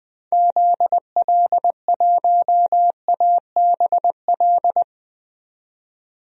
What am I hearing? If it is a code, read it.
Morse code, heard as a keyed tone, about 20 wpm, 700 Hz. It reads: ZL1ABL
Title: DXCC-100 Call Signs - 6 Characters - 20wpm